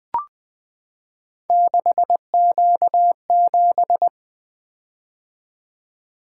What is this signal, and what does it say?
Morse code, heard as a keyed tone, about 21 wpm, 700 Hz. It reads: 6Q7
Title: Three Letter-Number Combinations - 20wpm